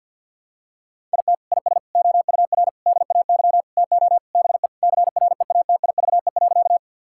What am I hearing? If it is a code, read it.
Morse code, heard as a keyed tone, about 50 wpm, 700 Hz. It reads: IT IS OUR DAY TO BE CREATIVE.
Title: Sentences from Top 1000 Words - Review Part 5 - 50wpm